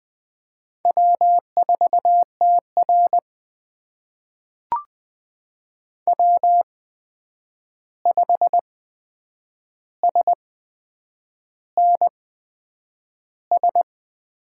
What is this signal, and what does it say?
Morse code, heard as a keyed tone, about 20 wpm, spaced as for 8 wpm, 700 Hz. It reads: W4TR W5SNS
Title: USA Call Signs - 8x Character Spacing - 20wpm